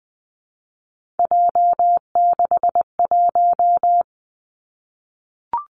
Morse code, heard as a keyed tone, about 20 wpm, 700 Hz. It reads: J61
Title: Three Challenging Letter-Number Combinations 20wpm